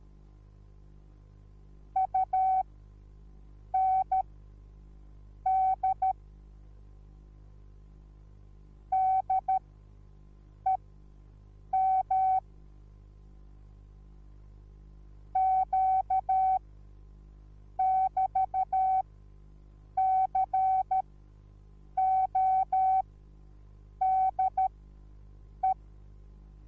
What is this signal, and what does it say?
Morse code, heard as a keyed tone, about 13 wpm, spaced as for 6 wpm, 760 Hz. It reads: UND DEM Q<BT>CODE